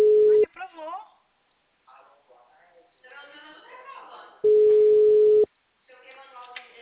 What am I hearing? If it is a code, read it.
Ringback tone (Europe, 425 Hz), with speech.
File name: out- -2004